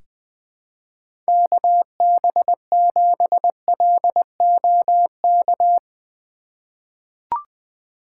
Morse code, heard as a keyed tone, about 21 wpm, 700 Hz. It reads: KB7LOK
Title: USA Call Signs - 6 Characters - 4x Character Spacing - 20wpm